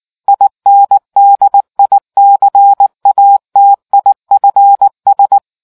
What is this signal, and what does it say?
Morse code, heard as a keyed tone, about 19 wpm, 800 Hz. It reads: INDICATIFS